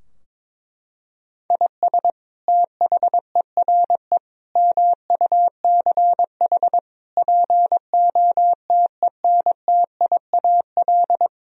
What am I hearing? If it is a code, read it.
Morse code, heard as a keyed tone, about 22 wpm, 700 Hz. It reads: IS THERE MUCH POTENTIAL
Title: Sentences from Top 1000 Words - Review Part 4 - 22wpm